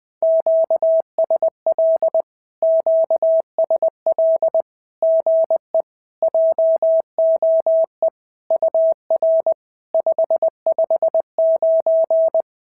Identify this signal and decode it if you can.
Morse code, heard as a keyed tone, about 20 wpm, 650 Hz. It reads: QSL QSL GE JOE UR 559
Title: SOTA QSO Practice 20wpm